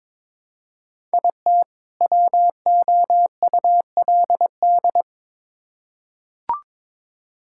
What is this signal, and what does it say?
Morse code, heard as a keyed tone, about 22 wpm, 700 Hz. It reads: IT WOULD
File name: Build A Phrase - 2-Word Phrases - 22wpm